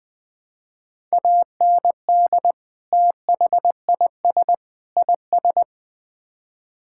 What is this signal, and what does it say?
Morse code, heard as a keyed tone, about 20 wpm, 700 Hz. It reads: AND THIS IS
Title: Build A Phrase - 3-Word Phrases - 20wpm